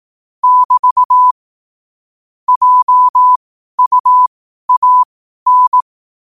Morse code, heard as a keyed tone, about 18 wpm, spaced as for 12 wpm, 1000 Hz. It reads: <BT> JUAN
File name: qotdfiles / 12wpm